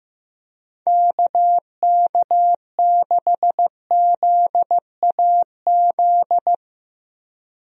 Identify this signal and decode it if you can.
Morse code, heard as a keyed tone, about 15 wpm, 700 Hz. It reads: KK6ZAZ
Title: USA Call Signs - 6 Characters - 15wpm